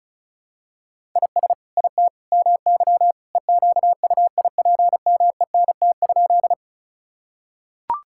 Morse code, heard as a keyed tone, about 36 wpm, 690 Hz. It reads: IS IT MY EQUIPMENT?
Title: Sentences from Top 1000 Words - Review Part 5 - 35wpm